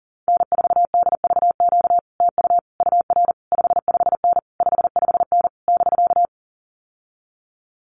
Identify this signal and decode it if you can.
Morse code, heard as a keyed tone, about 40 wpm, 700 Hz. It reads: N4DVQ TU UR 55N 55N <BK>
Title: Common QSO Phrases Encore 40wpm